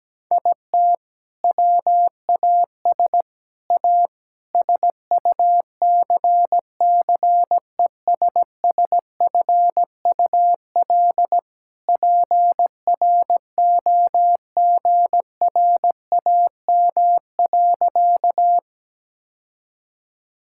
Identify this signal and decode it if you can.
Morse code, heard as a keyed tone, about 17 wpm, 700 Hz. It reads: IT WAS A SUCCESSFUL PROGRAM.
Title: Sentences from Top 1000 Words - Review Part 5 - 17wpm